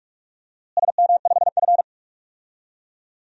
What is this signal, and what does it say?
Morse code, heard as a keyed tone, about 41 wpm, 700 Hz. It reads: SM5F
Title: DXCC-100 Call Signs - 4 Characters - Repeated 2x - 45wpm